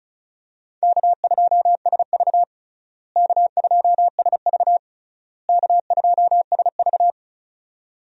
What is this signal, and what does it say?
Morse code, heard as a keyed tone, about 34 wpm, 700 Hz. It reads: K2SV K2SV K2SV
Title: USA Call Signs - 4 Characters - Repeated 3x - 35wpm